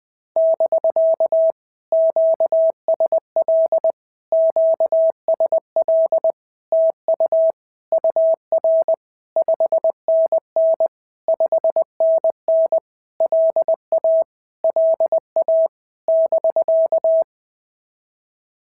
Morse code, heard as a keyed tone, about 19 wpm, 650 Hz. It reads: <BK> QSL QSL TU UR 5NN 5NN LA LA <BK>
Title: POTA QSO Practice - 20wpm